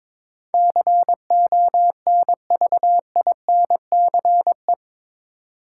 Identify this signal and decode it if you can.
Morse code, heard as a keyed tone, about 22 wpm, 700 Hz. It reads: CONVINCE
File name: practice / 22wpm / Eight Letter Words - 22wpm